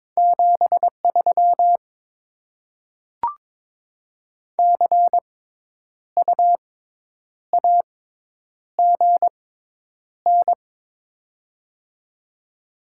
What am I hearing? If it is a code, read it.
Morse code, heard as a keyed tone, about 23 wpm, spaced as for 9 wpm, 690 Hz. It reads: 73 CU AGN
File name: Top 20 QSO Elements - 6x Character Spacing - 22wpm